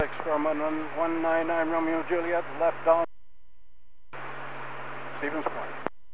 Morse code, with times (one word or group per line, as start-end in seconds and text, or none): none